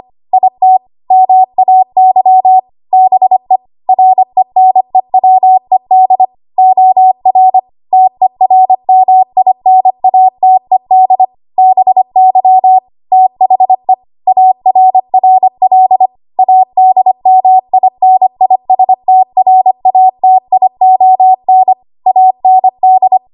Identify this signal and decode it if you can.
Morse code, heard as a keyed tone, about 25 wpm, 750 Hz. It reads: IT MAY BE RENEWED OR TERMINATED BY THE ARRL ADMINISTRATION AND
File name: Archive / 25 WPM / 25WPM